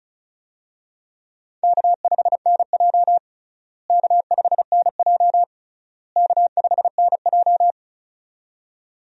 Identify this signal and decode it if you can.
Morse code, heard as a keyed tone, about 36 wpm, 700 Hz. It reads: K5NJ K5NJ K5NJ
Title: USA Call Signs - 4 Characters - Repeated 3x - 35wpm